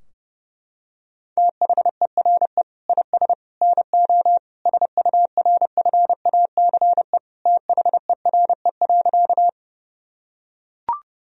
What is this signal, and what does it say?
Morse code, heard as a keyed tone, about 30 wpm, 700 Hz. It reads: THERE IS NO SURFACE THERE.
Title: Sentences from Top 1000 Words - Review Part 3 - 30wpm